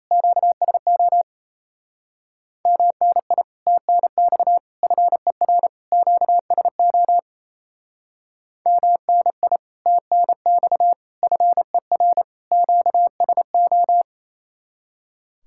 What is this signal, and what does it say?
Morse code, heard as a keyed tone, about 32 wpm, 700 Hz. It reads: QSO MNI TNX FER QSO MNI TNX FER QSO
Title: Speed-Racing Common QSO Phrases - 28wpm